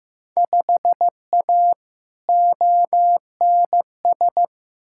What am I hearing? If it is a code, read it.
Morse code, heard as a keyed tone, about 15 wpm, 700 Hz. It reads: 5A ONS
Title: Field Day Exchanges Encore - 15wpm